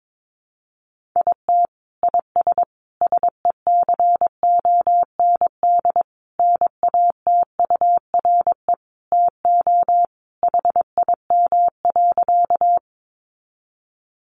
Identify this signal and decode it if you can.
Morse code, heard as a keyed tone, about 22 wpm, 700 Hz. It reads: IT IS SECOND NATURE TO HIM.
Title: Sentences from Top 600 Words 22wpm